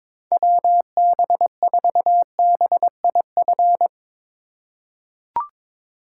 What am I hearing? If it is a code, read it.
Morse code, heard as a keyed tone, about 22 wpm, 700 Hz. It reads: WB4BIF
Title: USA Call Signs - 6 Characters - 3x Character Spacing - 22wpm